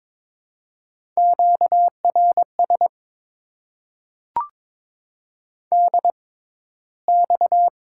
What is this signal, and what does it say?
Morse code, heard as a keyed tone, about 22 wpm, spaced as for 13 wpm, 700 Hz. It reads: QRS DX
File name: Top 90 QSO Elements - 6x Character Spacing - 22wpm